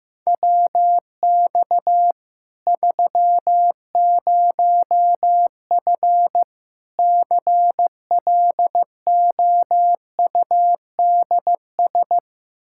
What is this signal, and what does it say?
Morse code, heard as a keyed tone, about 15 wpm, 700 Hz. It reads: WX 30F CLOUDS
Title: Practice QSO - Exchange 2 - Info - 15wpm